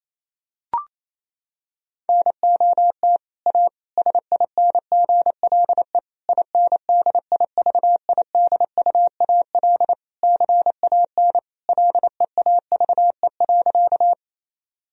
Morse code, heard as a keyed tone, about 28 wpm, 700 Hz. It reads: NOT A SINGLE INDIVIDUAL CAN LEAVE.
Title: Sentences from Top 1000 Words - Review Part 4 - 28wpm